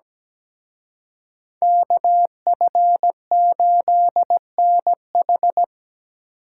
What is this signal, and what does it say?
Morse code, heard as a keyed tone, about 20 wpm, 700 Hz. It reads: KF8NH
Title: Speed Racing - USA Call Signs 17wpm